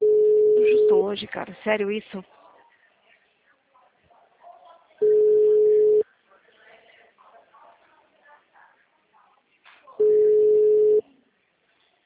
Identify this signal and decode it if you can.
Ringback tone (Europe, 425 Hz), with speech.
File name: out- -2005- - - .45171